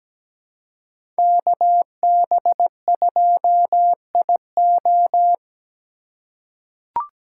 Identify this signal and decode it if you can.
Morse code, heard as a keyed tone, about 17 wpm, 700 Hz. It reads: KB2IO
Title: USA Call Signs - 5 Characters - 6x Character Spacing - 17wpm